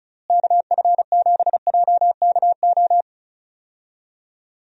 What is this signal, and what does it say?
Morse code, heard as a keyed tone, about 36 wpm, 700 Hz. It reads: KF7JKO
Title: USA Call Signs - 6 Characters - Repeated 2x - 35wpm